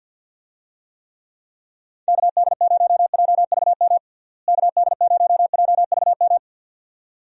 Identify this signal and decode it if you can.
Morse code, heard as a keyed tone, about 50 wpm, 700 Hz. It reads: KD0JVM KD0JVM
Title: USA Call Signs - 6 Characters - Repeated 2x - 50wpm